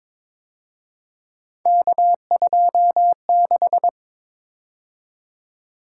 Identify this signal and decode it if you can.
Morse code, heard as a keyed tone, about 26 wpm, 700 Hz. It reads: K26
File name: Speed-Racing - Three Letter-Number Combinations - 22wpm